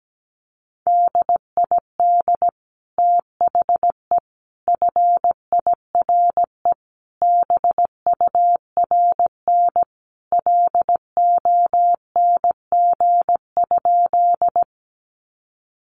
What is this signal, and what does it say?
Morse code, heard as a keyed tone, about 17 wpm, 700 Hz. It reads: DID THE FIRE BURN LONG?
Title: Sentences from Top 1300 Words 17wpm